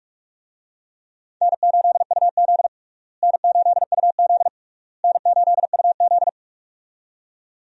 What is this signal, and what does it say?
Morse code, heard as a keyed tone, about 46 wpm, 700 Hz. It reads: N8UZ N8UZ N8UZ
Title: USA Call Signs - 4 Characters - Repeated 3x - 45wpm